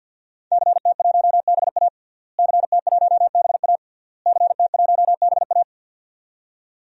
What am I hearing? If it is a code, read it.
Morse code, heard as a keyed tone, about 49 wpm, 700 Hz. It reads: CT1BA CT1BA CT1BA
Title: DXCC-100 Call Signs - 5 Characters - Repeated 3x - 50wpm